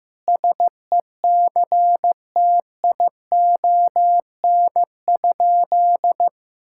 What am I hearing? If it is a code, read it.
Morse code, heard as a keyed tone, about 15 wpm, 700 Hz. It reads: SECTION?
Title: Sentences from Top 1000 Words - Review Part 5 - 15wpm